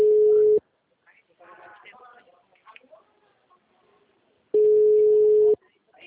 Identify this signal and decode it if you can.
Ringback tone (Europe, 425 Hz), with speech.